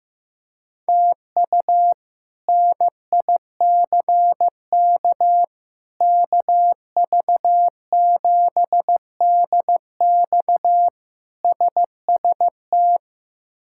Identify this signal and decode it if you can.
Morse code, heard as a keyed tone, about 15 wpm, 700 Hz. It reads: TU NICK KV7DX SST
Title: K1USN SST QSO Practice - 15wpm